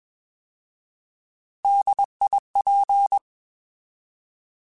Morse code, heard as a keyed tone, about 21 wpm, 800 Hz. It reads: DIP
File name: Alphabetical / Alphabetical mp3s / Ds @20